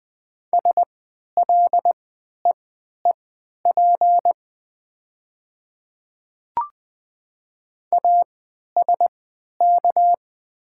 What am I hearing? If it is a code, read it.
Morse code, heard as a keyed tone, about 20 wpm, spaced as for 11 wpm, 700 Hz. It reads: SLEEP ASK
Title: 3x Character Spacing - Top 500 Words - 20wpm